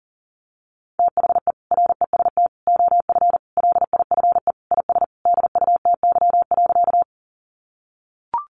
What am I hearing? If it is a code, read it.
Morse code, heard as a keyed tone, about 40 wpm, 700 Hz. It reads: THE REST OF LIFE IS DUTY.
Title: Sentences from Top 1500 Words 40wpm